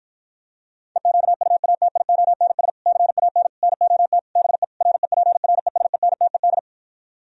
Morse code, heard as a keyed tone, about 47 wpm, 700 Hz. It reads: EQUATIONS CAN NOT BE REPRESENTED